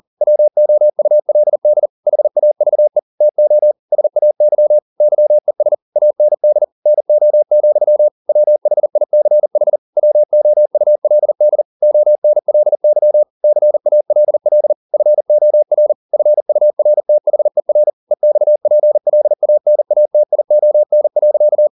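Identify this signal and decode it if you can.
Morse code, heard as a keyed tone, about 40 wpm, 600 Hz. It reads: WOULD HAVE TO SAY YES AND NO, WHICH WOULD ONLY CALL FOR FURTHER EXPLANATION.
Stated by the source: Morse code at 40 wpm, practice text, CW med Gnister.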